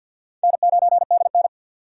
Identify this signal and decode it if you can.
Morse code, heard as a keyed tone, about 47 wpm, 700 Hz. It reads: N9DN
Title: USA Call Signs - 2x Character Spacing - 50wpm